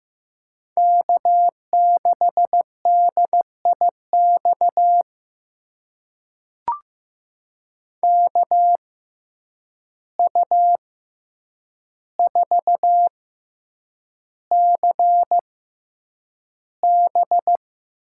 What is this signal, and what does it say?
Morse code, heard as a keyed tone, about 15 wpm, spaced as for 8 wpm, 700 Hz. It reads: K6DIX KU4CB
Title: USA Call Signs - 5 Characters - 6x Character Spacing - 15wpm